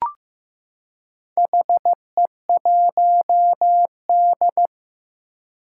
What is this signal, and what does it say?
Morse code, heard as a keyed tone, about 15 wpm, 700 Hz. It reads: HE1D
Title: Four Challenging Letter-Number Combinations 15wpm